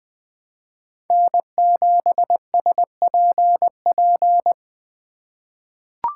Morse code, heard as a keyed tone, about 20 wpm, 700 Hz. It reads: N7SPP
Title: USA Call Signs - 5 Characters - 4x Character Spacing - 20wpm